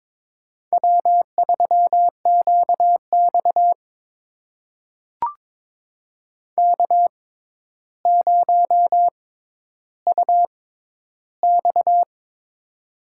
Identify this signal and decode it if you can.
Morse code, heard as a keyed tone, about 22 wpm, spaced as for 11 wpm, 700 Hz. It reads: W3QX K0UX
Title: USA Call Signs - 6x Character Spacing - 22wpm